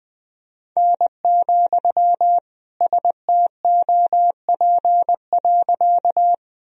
Morse code, heard as a keyed tone, about 20 wpm, 700 Hz. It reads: N, STOP.
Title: Sentences from Top 300 Words and Abbreviations 20wpm